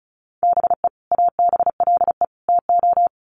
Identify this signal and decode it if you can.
Morse code, heard as a keyed tone, about 34 wpm, 700 Hz. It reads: BE ABLE TO
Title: Sets of 3 Words 35wpm